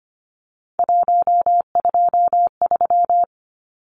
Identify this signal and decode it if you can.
Morse code, heard as a keyed tone, about 25 wpm, 700 Hz. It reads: 123
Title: Numbers - 3-Digit 25wpm